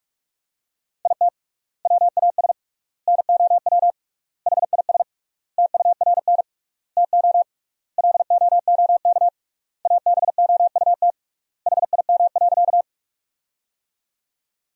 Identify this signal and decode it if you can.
Morse code, heard as a keyed tone, about 46 wpm, 700 Hz. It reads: IT WAS NOW HIS TURN TO LOOK ABOUT HIM.
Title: 3x Word-Spacing - Sentences from Top 200 Words - 45wpm